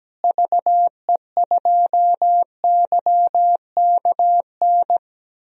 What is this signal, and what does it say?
Morse code, heard as a keyed tone, about 17 wpm, 700 Hz. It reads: VE2YKN
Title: DXCC-100 Call Signs - 6 Characters - Repeated 3x - 17wpm